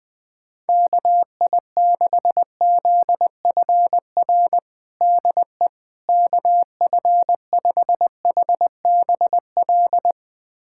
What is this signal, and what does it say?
Morse code, heard as a keyed tone, about 20 wpm, 700 Hz. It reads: KI6ZFR DE KF5HBL
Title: USA Callsign DE Callsign - 6 Characters - 20wpm